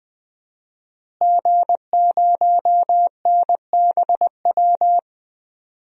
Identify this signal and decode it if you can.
Morse code, heard as a keyed tone, about 21 wpm, 700 Hz. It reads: G0NBW
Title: DXCC-100 Call Signs - 5 Characters - Repeated 3x - 20wpm